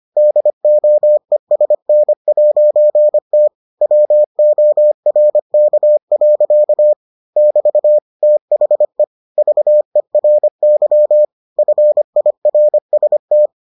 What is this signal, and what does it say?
Morse code, heard as a keyed tone, about 25 wpm, 600 Hz. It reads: DOESN'T WORK. <BT> THE VERY FIRST